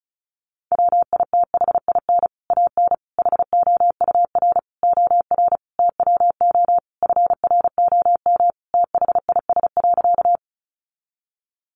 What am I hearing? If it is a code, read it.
Morse code, heard as a keyed tone, about 34 wpm, 700 Hz. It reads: WITHIN AN HOUR OR TWO FROM THIS.
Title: Sentences from Top 700 Words 35wpm